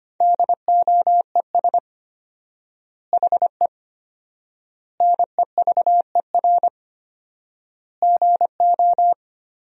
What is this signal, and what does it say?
Morse code, heard as a keyed tone, about 25 wpm, 700 Hz. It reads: DOES HE NEVER GO
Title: 4x Word-Spacing - Sentences from Top 300 Words - 25wpm